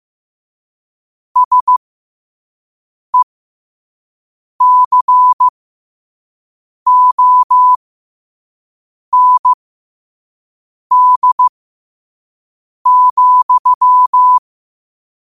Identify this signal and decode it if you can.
Morse code, heard as a keyed tone, about 15 wpm, spaced as for 5 wpm, 1000 Hz. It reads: SECOND,
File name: qotdfiles / 5wpm